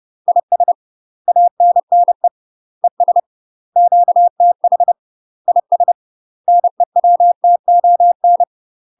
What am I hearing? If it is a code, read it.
Morse code, heard as a keyed tone, about 30 wpm, 700 Hz. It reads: IS ANNE ES QTH IS NEWTON
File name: UK MASTER QSOs TRAINING / 30WPM / UK MASTER QSOs 0002